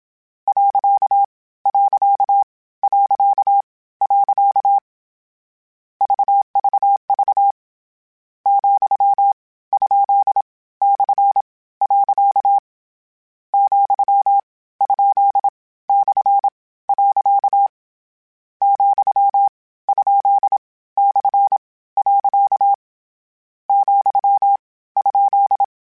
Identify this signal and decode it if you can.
Morse code, heard as a keyed tone, about 27 wpm, 800 Hz. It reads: . . VVV , ? / . , ? / . , ? / . , ?
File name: Misc / Misc-mp3s / Punct @25